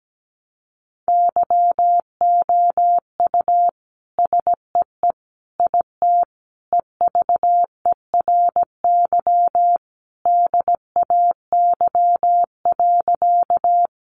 Morse code, heard as a keyed tone, about 17 wpm, 700 Hz. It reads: YOU SEE IT EVERY DAY.
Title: Sentences from Top 200 Words 17wpm